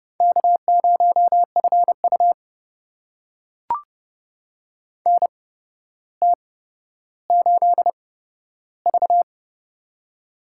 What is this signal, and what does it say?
Morse code, heard as a keyed tone, about 29 wpm, spaced as for 13 wpm, 700 Hz. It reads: K0FU NT8V
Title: USA Call Signs - 4 Characters - 8x Character Spacing - 30wpm